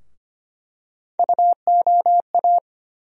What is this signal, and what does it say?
Morse code, heard as a keyed tone, about 25 wpm, 700 Hz. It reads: UOA
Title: USA Call Sign Postfixes - 3 Letters 25wpm